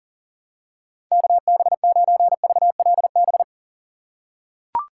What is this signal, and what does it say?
Morse code, heard as a keyed tone, about 40 wpm, 700 Hz. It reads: KB9VLB
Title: USA Call Signs - 6 Characters - 4x Character Spacing - 40wpm